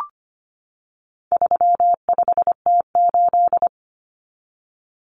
Morse code, heard as a keyed tone, about 25 wpm, 700 Hz. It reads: 35T8
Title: Common Cut Numbers - 4-Digit 25wpm